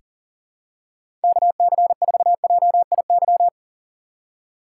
Morse code, heard as a keyed tone, about 40 wpm, 700 Hz. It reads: KC4JIY
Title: USA Call Signs - 6 Characters - 4x Character Spacing - 40wpm